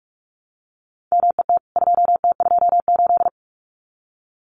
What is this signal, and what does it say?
Morse code, heard as a keyed tone, about 44 wpm, 700 Hz. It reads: MET 2T28
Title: CWT Exchange 45wpm